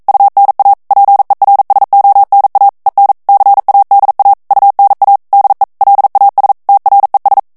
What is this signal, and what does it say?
Morse code, heard as a keyed tone, about 42 wpm, 800 Hz. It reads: UNA PERSONA EN CADA UNA DE LAS TRES
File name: morse mp3 / ESPAGNOL TEXTE EN CLAIR 1 VITESSE 40 MOTS MINUTES